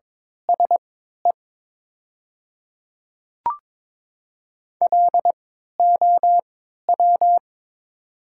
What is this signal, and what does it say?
Morse code, heard as a keyed tone, about 22 wpm, spaced as for 13 wpm, 700 Hz. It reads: SE LOW
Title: 3x Character Spacing - Top 300 Words - 22wpm